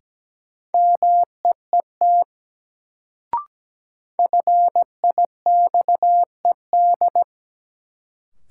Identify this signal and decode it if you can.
Morse code, heard as a keyed tone, about 17 wpm, 700 Hz. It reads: MEET FIXED
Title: Sentences from Top 1100 Words - New Word Drill Alt - 17wpm